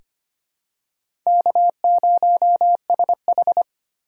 Morse code, heard as a keyed tone, about 25 wpm, 700 Hz. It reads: K0SH
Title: USA Call Signs - 3x Character Spacing - 25wpm